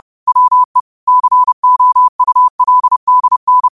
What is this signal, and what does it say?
Morse code, heard as a keyed tone, about 30 wpm, 1000 Hz. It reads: WE COULDN
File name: qotdfiles / 30wpm